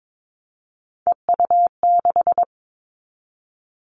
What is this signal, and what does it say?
Morse code, heard as a keyed tone, about 22 wpm, 700 Hz. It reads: EU6
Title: Callsign Prefixes for Top 100 DXCC - 22wpm